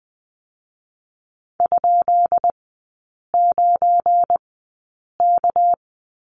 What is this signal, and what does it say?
Morse code, heard as a keyed tone, about 20 wpm, spaced as for 8 wpm, 700 Hz. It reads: ?9K
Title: 198 Lesson 32e - Letter k - Three Characters 20wpm